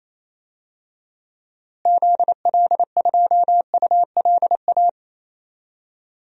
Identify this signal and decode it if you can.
Morse code, heard as a keyed tone, about 28 wpm, 700 Hz. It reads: ZL2ULA
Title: DXCC-100 Call Signs - 6 Characters - Repeated 2x - 28wpm